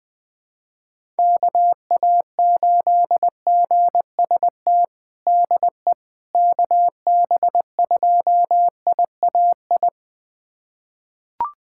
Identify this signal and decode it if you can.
Morse code, heard as a keyed tone, about 21 wpm, 700 Hz. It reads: KA8GST DE KB2IAI
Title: USA Callsign DE Callsign - 6 Characters - 20wpm